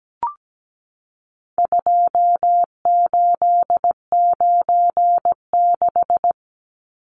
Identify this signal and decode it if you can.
Morse code, heard as a keyed tone, about 17 wpm, 700 Hz. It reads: 2896
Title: Numbers - 4-Digit 17wpm